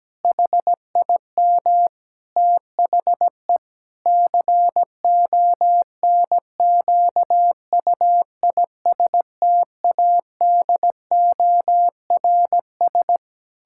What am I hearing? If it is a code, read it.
Morse code, heard as a keyed tone, about 17 wpm, 700 Hz. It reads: HIM THE CONQUISTADORS
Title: Pangram Challenge v2 - 17wpm